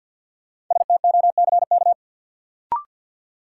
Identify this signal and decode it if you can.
Morse code, heard as a keyed tone, about 50 wpm, 700 Hz. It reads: STOCK
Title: Sets of 5 Words - New Word Drill - 50wpm